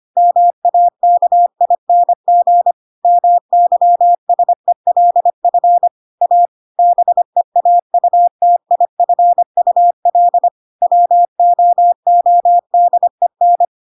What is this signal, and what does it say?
Morse code, heard as a keyed tone, about 25 wpm, 700 Hz. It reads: MAKING MYSELF A BEAUTIFUL WOODEN
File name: Pinocchio / Pinocchio ch2 25 25 ik1hge